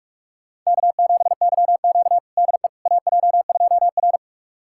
Morse code, heard as a keyed tone, about 45 wpm, 700 Hz. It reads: K7YQ DE AJ2R
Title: USA Callsign DE Callsign - 4 Characters - 45wpm